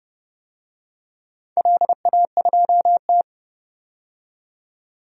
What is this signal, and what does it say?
Morse code, heard as a keyed tone, about 30 wpm, 700 Hz. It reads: LA2T
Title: DXCC-100 Call Signs - 4 Characters - 30wpm